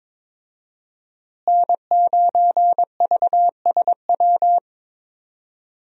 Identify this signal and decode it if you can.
Morse code, heard as a keyed tone, about 22 wpm, 700 Hz. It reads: N9VSW
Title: USA Call Signs - 5 Characters - Repeated 2x - 22wpm